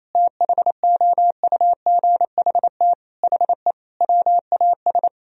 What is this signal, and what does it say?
Morse code, heard as a keyed tone, about 27 wpm, 700 Hz. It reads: THOUGHT HE WAS
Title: Sentences from Top 1000 Words - Review Part 3 - 28wpm